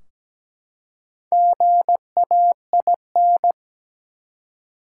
Morse code, heard as a keyed tone, about 17 wpm, 700 Hz. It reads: GAIN
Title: Four Letter Words - 17wpm